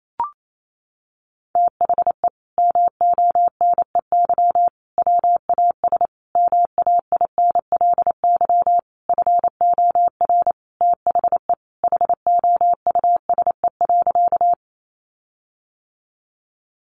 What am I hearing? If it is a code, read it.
Morse code, heard as a keyed tone, about 28 wpm, 700 Hz. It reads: THE MONEY WAS MAINLY FOR THE HOUSE.
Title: Sentences from Top 1400 Words 28wpm